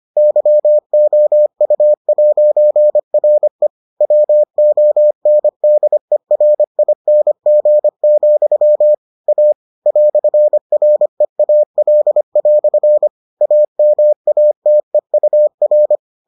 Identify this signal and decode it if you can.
Morse code, heard as a keyed tone, about 25 wpm, 600 Hz. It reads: YOU'RE WONDERING, A "REAL" AMATEUR